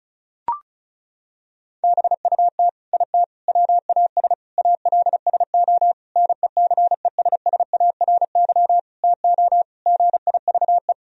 Morse code, heard as a keyed tone, about 36 wpm, 700 Hz. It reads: BUT IT WAS ALSO NECESSARY TO GIVE
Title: Sentences from Top 1000 Words - Review Part 2 - 35wpm